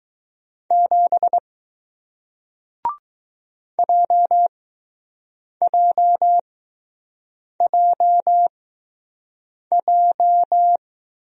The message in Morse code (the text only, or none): 7 JJJJ